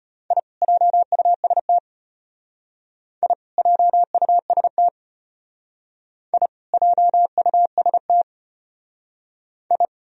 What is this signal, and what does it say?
Morse code, heard as a keyed tone, about 33 wpm, 700 Hz. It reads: I JUST I JUST I JUST I